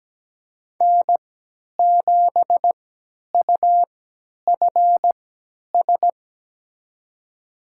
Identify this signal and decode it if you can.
Morse code, heard as a keyed tone, about 17 wpm, spaced as for 10 wpm, 700 Hz. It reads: N7UFS
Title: USA Call Signs - 5 Characters - 3x Character Spacing - 17wpm